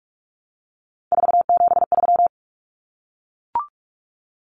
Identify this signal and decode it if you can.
Morse code, heard as a keyed tone, about 44 wpm, 700 Hz. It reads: 473
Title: Numbers - 3-Digit 45wpm